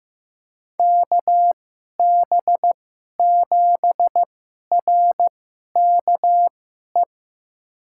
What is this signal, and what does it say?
Morse code, heard as a keyed tone, about 15 wpm, spaced as for 11 wpm, 700 Hz. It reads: KB7RKE
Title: USA Call Signs - 6 Characters - 2x Character Spacing - 15wpm